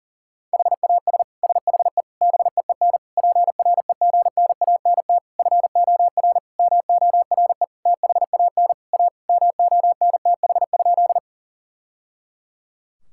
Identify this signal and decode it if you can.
Morse code, heard as a keyed tone, about 40 wpm, 700 Hz. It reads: HAS SHE BEEN PREGNANT FOR MORE THAN A MONTH?
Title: Sentences from Top 1000 Words - Review Part 6 - 40wpm